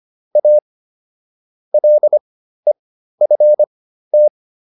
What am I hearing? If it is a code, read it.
Morse code, heard as a keyed tone, about 25 wpm, spaced as for 13 wpm, 600 Hz. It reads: A LEFT